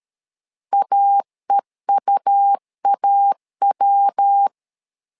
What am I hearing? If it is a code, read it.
Morse code, heard as a keyed tone, about 13 wpm, 800 Hz. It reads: AEUAW
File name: LEZIONI / 3 MWRO / 3 MW 12wpm 1633